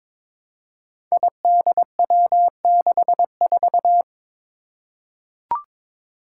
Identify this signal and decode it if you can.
Morse code, heard as a keyed tone, about 22 wpm, 700 Hz. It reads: IDW64
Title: Five Challenging Letter-Number Combinations 22wpm